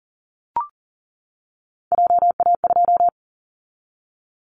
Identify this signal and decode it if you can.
Morse code, heard as a keyed tone, about 40 wpm, 700 Hz. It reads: JA2
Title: Callsign Prefixes for Top 50 DXCC - 40wpm